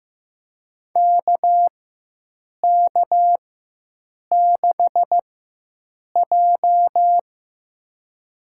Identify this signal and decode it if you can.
Morse code, heard as a keyed tone, about 15 wpm, spaced as for 7 wpm, 700 Hz. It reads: KK6J
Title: USA Call Signs - 4 Characters - 4x Character Spacing - 15wpm